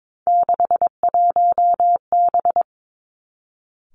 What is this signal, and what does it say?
Morse code, heard as a keyed tone, about 22 wpm, 700 Hz. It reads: 61B
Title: Three Challenging Letter-Number Combinations 22wpm